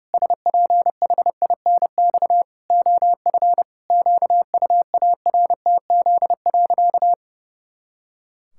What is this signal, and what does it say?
Morse code, heard as a keyed tone, about 32 wpm, 700 Hz. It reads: SPHINX OF QUARTZ.
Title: Pangram Challenge v2 - 30wpm